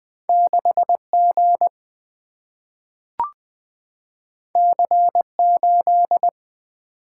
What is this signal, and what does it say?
Morse code, heard as a keyed tone, about 20 wpm, 700 Hz. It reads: 6G C8
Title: Single-Letter Single-Number 20wpm